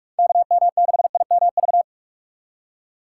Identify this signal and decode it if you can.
Morse code, heard as a keyed tone, about 46 wpm, 700 Hz. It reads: KM6IMV
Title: USA Call Signs - 6 Characters - 2x Character Spacing - 45wpm